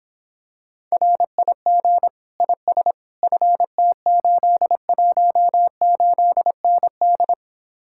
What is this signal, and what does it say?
Morse code, heard as a keyed tone, about 26 wpm, 700 Hz. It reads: RIG IS FT818ND